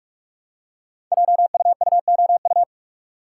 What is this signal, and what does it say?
Morse code, heard as a keyed tone, about 44 wpm, 700 Hz. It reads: JUUOU
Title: Code Groups - 5 Letters - 45wpm